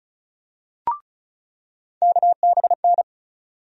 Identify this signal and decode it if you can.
Morse code, heard as a keyed tone, about 36 wpm, 700 Hz. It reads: KBN
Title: USA Call Sign Postfixes - 3 Letters 35wpm